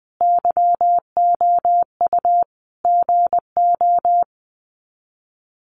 Morse code, heard as a keyed tone, about 20 wpm, 700 Hz. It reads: YOU GO
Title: Sets of 5 Words Encore 20wpm